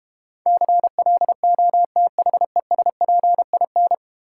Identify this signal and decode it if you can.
Morse code, heard as a keyed tone, about 31 wpm, 700 Hz. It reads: CLOTHESPIN